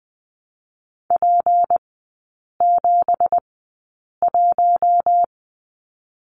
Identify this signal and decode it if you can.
Morse code, heard as a keyed tone, about 20 wpm, spaced as for 8 wpm, 700 Hz. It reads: P71
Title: 218 Lesson 35e - Number 0 - Three Characters 20wpm